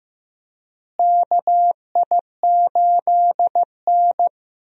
Morse code, heard as a keyed tone, about 15 wpm, 700 Hz. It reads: KI8N
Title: USA Call Signs - 4 Characters - Repeated 2x - 15wpm